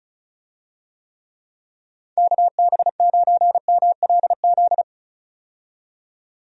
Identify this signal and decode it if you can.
Morse code, heard as a keyed tone, about 36 wpm, 700 Hz. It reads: KB9MLZ
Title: USA Call Signs - 6 Characters - 35wpm